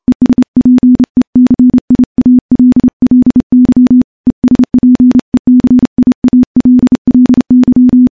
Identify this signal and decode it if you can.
Morse code, heard as a keyed tone, about 30 wpm, 250 Hz. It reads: ESPECIALLY ESPECIALLY
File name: morse / morse-training-30WPM